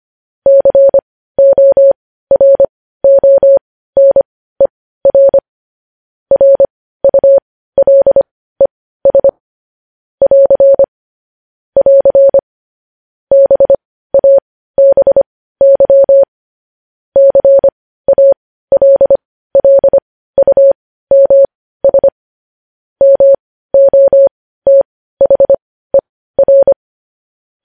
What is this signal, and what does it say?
Morse code, heard as a keyed tone, about 25 wpm, spaced as for 15 wpm, 550 Hz. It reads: CORONER RULES <AR> <AR> BABY CALLUMS MOTHER
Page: News Headlines in Morse Code at 15 WPM